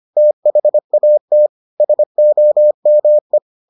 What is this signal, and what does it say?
Morse code, heard as a keyed tone, about 25 wpm, 600 Hz. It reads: THAT SOME